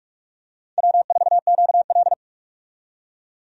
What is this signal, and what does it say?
Morse code, heard as a keyed tone, about 42 wpm, 700 Hz. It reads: W4QL
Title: USA Call Signs - 4 Characters - 45wpm